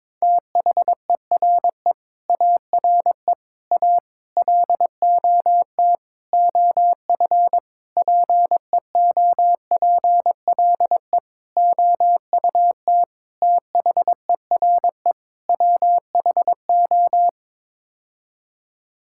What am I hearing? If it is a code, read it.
Morse code, heard as a keyed tone, about 22 wpm, 700 Hz. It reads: THERE ARE A LOT OF PEOPLE OUT THERE WHO
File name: Sets of 9 Words - 22wpm